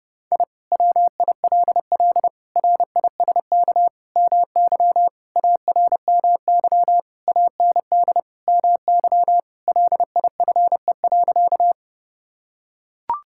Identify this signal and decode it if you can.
Morse code, heard as a keyed tone, about 30 wpm, 700 Hz. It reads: I WILL RISK MY ARMY AND MY LIFE.
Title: Sentences from Top 1000 Words - Review Part 5 - 30wpm